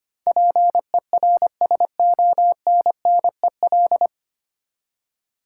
Morse code, heard as a keyed tone, about 25 wpm, 700 Hz. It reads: PERSONNEL
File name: practice / 25wpm / 9 Letter Words - First 3 Letters Given - 25wpm